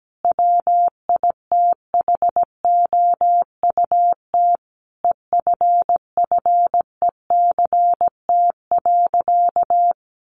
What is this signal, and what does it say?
Morse code, heard as a keyed tone, about 17 wpm, 700 Hz. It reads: WITHOUT EFFECT.
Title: Sentences from Top 1400 Words 17wpm